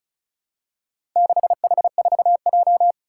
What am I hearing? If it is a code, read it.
Morse code, heard as a keyed tone, about 34 wpm, 700 Hz. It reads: 6H4J
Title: Four Challenging Letter-Number Combinations 35wpm